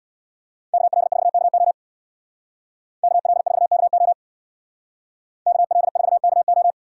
Morse code, heard as a keyed tone, about 61 wpm, 700 Hz. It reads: KL4KO KL4KO KL4KO
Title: Speed Racing - USA 2x2 Call Signs 50wpm